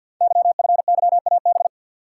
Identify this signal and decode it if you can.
Morse code, heard as a keyed tone, about 45 wpm, 700 Hz. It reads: YFYAB
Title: Code Groups - 5 Letter:Numbers - 50wpm